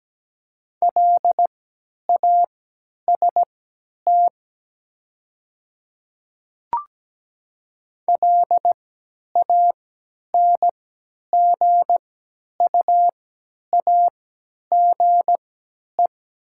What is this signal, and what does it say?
Morse code, heard as a keyed tone, about 17 wpm, spaced as for 10 wpm, 690 Hz. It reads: LAST LANGUAGE
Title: 3x Character Spacing - Top 500 Words - 17wpm